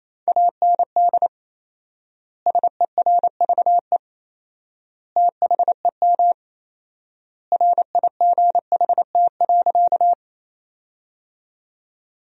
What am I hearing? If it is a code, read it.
Morse code, heard as a keyed tone, about 28 wpm, 700 Hz. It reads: AND SERVE THEM RIGHT.
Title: 4x Word-Spacing - Sentences from Top 400 Words - 28wpm